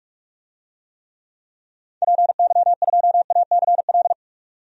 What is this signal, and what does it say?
Morse code, heard as a keyed tone, about 46 wpm, 700 Hz. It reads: PY2ACL
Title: DXCC-100 Call Signs - 6 Characters - 45wpm